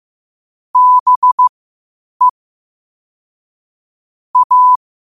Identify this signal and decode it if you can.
Morse code, heard as a keyed tone, about 15 wpm, spaced as for 8 wpm, 1000 Hz. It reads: BE A
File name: qotdfiles / 8wpm